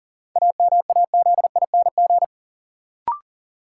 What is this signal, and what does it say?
Morse code, heard as a keyed tone, about 40 wpm, 700 Hz. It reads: AMAZING
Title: Sentences from Top 1500 Words - New Word Drill - 40wpm